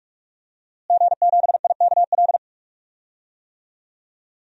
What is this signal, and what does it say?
Morse code, heard as a keyed tone, about 46 wpm, 700 Hz. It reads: G7IKL
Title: DXCC-100 Call Signs - 5 Characters - Repeated 2x - 45wpm